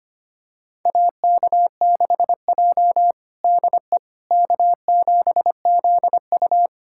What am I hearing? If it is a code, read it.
Morse code, heard as a keyed tone, about 25 wpm, 700 Hz. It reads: AK6J DE K7ZU
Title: USA Callsign DE Callsign - 4 Characters - 25wpm